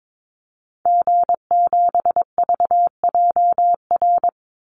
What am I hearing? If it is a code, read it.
Morse code, heard as a keyed tone, about 22 wpm, 700 Hz. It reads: G7VJR
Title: Call Signs Repeated 3x - 22wpm